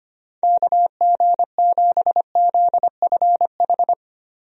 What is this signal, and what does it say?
Morse code, heard as a keyed tone, about 25 wpm, 700 Hz. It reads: KG7ZFH
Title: USA Call Signs - 6 Characters - Repeated 2x - 25wpm